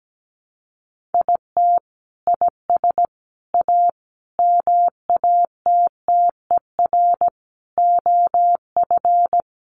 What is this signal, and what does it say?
Morse code, heard as a keyed tone, about 17 wpm, 700 Hz. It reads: IT IS A MATTER OF
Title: Sentences from Top 1400 Words 17wpm